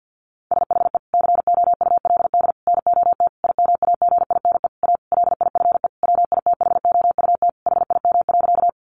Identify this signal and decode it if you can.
Morse code, heard as a keyed tone, about 50 wpm, 700 Hz. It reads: SHE COULD NOT IMAGINE A LIFE WITHOUT HIM.
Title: Sentences from Top 1400 Words 50wpm